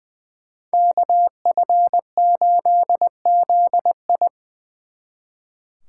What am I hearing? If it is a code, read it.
Morse code, heard as a keyed tone, about 24 wpm, 700 Hz. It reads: KF8ZI
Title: Speed Racing - USA Call Signs 20wpm